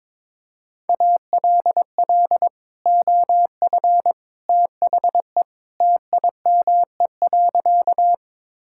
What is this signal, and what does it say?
Morse code, heard as a keyed tone, about 22 wpm, 700 Hz. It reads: ALL OF THE TIME.
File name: practice / 22wpm / Sentences with 3-Word Phrases - 22wpm